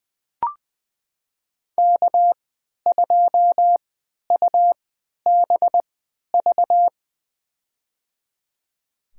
Morse code, heard as a keyed tone, about 20 wpm, spaced as for 11 wpm, 700 Hz. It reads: K2UBV
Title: USA Call Signs - 5 Characters - 3x Character Spacing - 20wpm